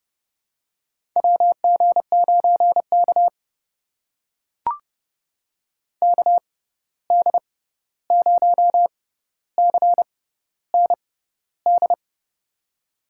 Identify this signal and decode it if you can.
Morse code, heard as a keyed tone, about 30 wpm, spaced as for 14 wpm, 700 Hz. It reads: WG9K KD0CND
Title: USA Call Signs - 6x Character Spacing - 30wpm